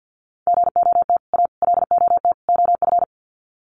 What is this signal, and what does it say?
Morse code, heard as a keyed tone, about 50 wpm, 700 Hz. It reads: GOT A LOT OF
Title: Sets of 4 Words 50wpm